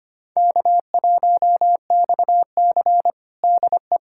Morse code, heard as a keyed tone, about 25 wpm, 700 Hz. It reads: K1XC DE
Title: USA Callsign DE Callsign - 4 Characters - 25wpm